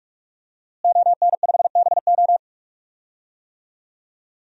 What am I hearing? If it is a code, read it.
Morse code, heard as a keyed tone, about 42 wpm, 700 Hz. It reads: ON5BO
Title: DXCC-100 Call Signs - 5 Characters - Repeated 2x - 45wpm